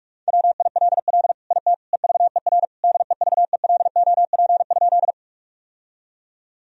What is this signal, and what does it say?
Morse code, heard as a keyed tone, about 46 wpm, 700 Hz. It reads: WILL IT EVER DEVELOP?
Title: Sentences from Top 1000 Words - Review Part 2 - 45wpm